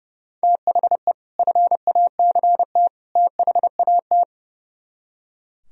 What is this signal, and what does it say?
Morse code, heard as a keyed tone, about 36 wpm, 700 Hz. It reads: THE FACT THAT
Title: Speed-Racing - Sets of 3 Words 30wpm